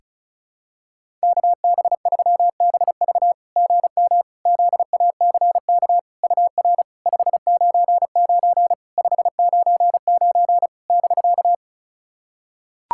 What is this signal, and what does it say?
Morse code, heard as a keyed tone, about 36 wpm, 690 Hz. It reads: KB3BV GM ZACK UR 599 599 <BK>
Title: SOTA QSO Practice 35wpm